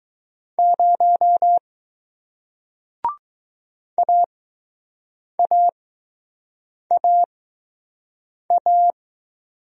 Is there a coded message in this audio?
Yes: Morse code from 0.6 to 8.9 s, heard as a keyed tone, about 20 wpm, spaced as for 6 wpm, 700 Hz. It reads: 0 AAAA